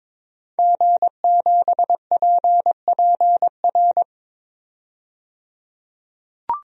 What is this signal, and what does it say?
Morse code, heard as a keyed tone, about 22 wpm, 700 Hz. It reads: G7PPR
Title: DXCC-100 Call Signs - 5 Characters - 22wpm